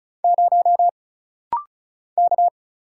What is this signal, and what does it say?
Morse code, heard as a keyed tone, about 36 wpm, 700 Hz. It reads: 0 K
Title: Single Letter-Number - Rapid-Fire 35wpm